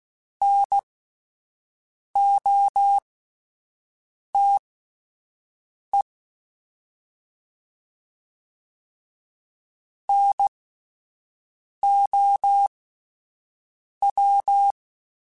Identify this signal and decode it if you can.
Morse code, heard as a keyed tone, about 16 wpm, spaced as for 5 wpm, 800 Hz. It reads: NOTE NOW
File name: Alphabetical / Alphabetical mp3s / Ns @5